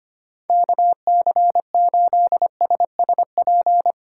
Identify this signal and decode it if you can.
Morse code, heard as a keyed tone, about 25 wpm, 700 Hz. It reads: KC8SSP
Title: USA Call Signs - 6 Characters - 4x Character Spacing - 25wpm